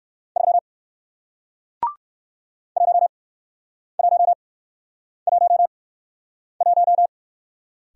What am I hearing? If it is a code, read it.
Morse code, heard as a keyed tone, about 56 wpm, spaced as for 9 wpm, 700 Hz. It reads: 3 1111
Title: Speed-Racing Single Letter-Number 45wpm